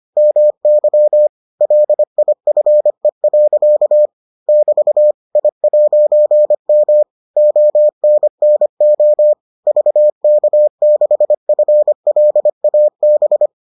Morse code, heard as a keyed tone, about 25 wpm, 600 Hz. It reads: MY LIFE. <BT> I'M ONNO VK6FLAB